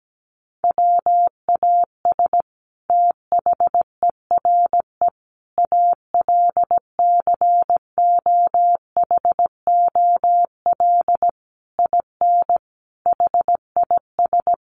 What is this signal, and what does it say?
Morse code, heard as a keyed tone, about 17 wpm, 700 Hz. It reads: WAS THERE ALCOHOL IN HIS
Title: Sentences from Top 1400 Words 17wpm